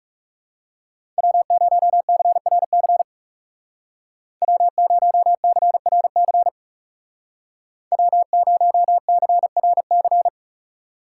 Morse code, heard as a keyed tone, about 39 wpm, 700 Hz. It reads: W0CRC W0CRC W0CRC